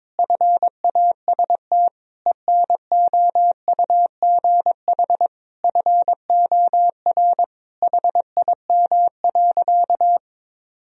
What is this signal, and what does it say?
Morse code, heard as a keyed tone, about 22 wpm, 700 Hz. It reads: FAST ENOUGH FOR HIM.
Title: Sentences from Top 1000 Words - Review Part 2 - 22wpm